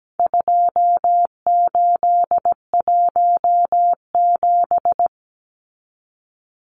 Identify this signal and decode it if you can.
Morse code, heard as a keyed tone, about 17 wpm, 700 Hz. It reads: 2817
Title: Numbers - 4-Digit 17wpm